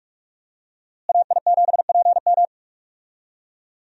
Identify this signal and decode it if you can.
Morse code, heard as a keyed tone, about 46 wpm, 700 Hz. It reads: AI7PM
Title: USA Call Signs - 5 Characters - Repeated 3x - 45wpm